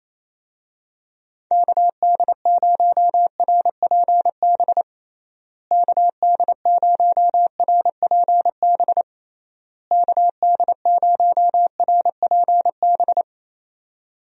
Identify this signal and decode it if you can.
Morse code, heard as a keyed tone, about 28 wpm, 700 Hz. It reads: KD0RPB KD0RPB KD0RPB
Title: USA Call Signs - 6 Characters - Repeated 3x - 28wpm